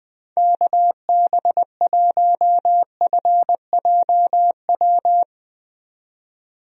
Morse code, heard as a keyed tone, about 20 wpm, 700 Hz. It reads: KB1FJW
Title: USA Call Signs - 6 Characters - 20wpm